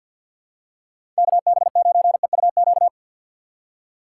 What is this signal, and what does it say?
Morse code, heard as a keyed tone, about 53 wpm, 700 Hz. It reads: KB9EUQ
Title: USA Call Signs - 6 Characters - 4x Character Spacing - 50wpm